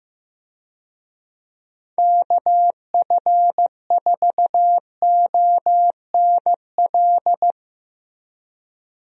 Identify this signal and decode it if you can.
Morse code, heard as a keyed tone, about 16 wpm, 700 Hz. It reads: KF4ONL
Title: USA Call Signs - 6 Characters - Repeated 2x - 15wpm